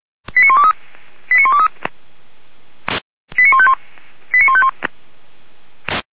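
Five-tone selective call (ZVEI), heard as quick runs of five tones, tones of about 70 ms. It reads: 89123 89123 89151 89151
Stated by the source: ZVEI five-tone selective call 89123 received by SDR, Feuerwehr Orlamünde fire service.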